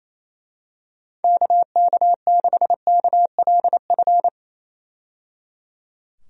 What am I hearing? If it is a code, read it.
Morse code, heard as a keyed tone, about 28 wpm, 700 Hz. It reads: KK6KLF
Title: USA Call Signs - 6 Characters - 28wpm